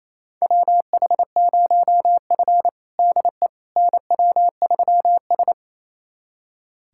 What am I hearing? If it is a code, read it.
Morse code, heard as a keyed tone, about 28 wpm, 700 Hz. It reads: WH0F DE NW3S
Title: USA Callsign DE Callsign - 4 Characters - 28wpm